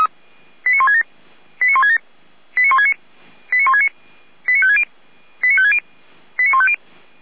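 Five-tone selective call (ZVEI), heard as quick runs of five tones, tones of about 70 ms. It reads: 89167 89167 89169 89169 89560 89560 89150
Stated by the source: ZVEI five-tone selective call received by SDR, from the Feuerwehr Orlamünde (f85360) archive.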